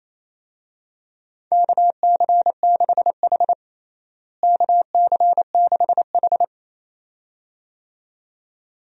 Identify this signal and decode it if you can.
Morse code, heard as a keyed tone, about 28 wpm, 700 Hz. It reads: KC6H KC6H
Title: USA Call Signs - 4 Characters - Repeated 2x - 28wpm